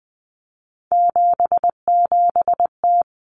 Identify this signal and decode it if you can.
Morse code, heard as a keyed tone, about 20 wpm, 700 Hz. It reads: 77T
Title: Common Cut Numbers - 3-Digit 20wpm